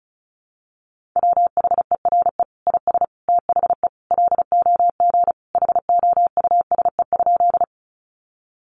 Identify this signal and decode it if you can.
Morse code, heard as a keyed tone, about 34 wpm, 700 Hz. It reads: WHERE IS THE LOG HOUSE?
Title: Sentences from Top 1400 Words 35wpm